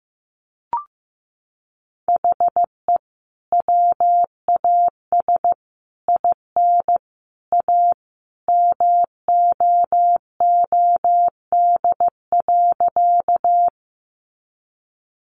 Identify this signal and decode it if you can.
Morse code, heard as a keyed tone, about 15 wpm, 700 Hz. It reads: HE WAS IN A MOOD.
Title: Sentences from Top 1100 Words 15wpm